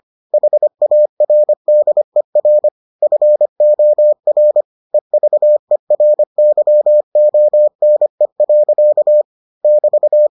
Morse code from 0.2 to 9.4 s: HARDER FOR EVERYONE.

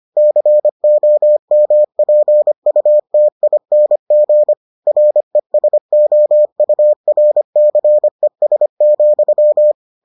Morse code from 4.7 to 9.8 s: RESOURCES,

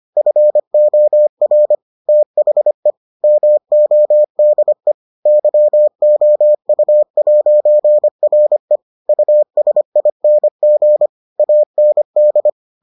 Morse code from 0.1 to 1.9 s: FOR

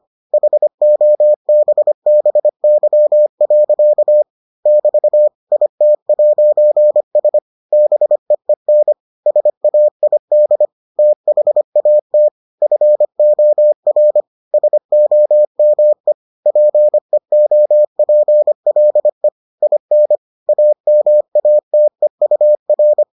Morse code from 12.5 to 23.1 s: FOR SOME PEOPLE IN AMATEUR

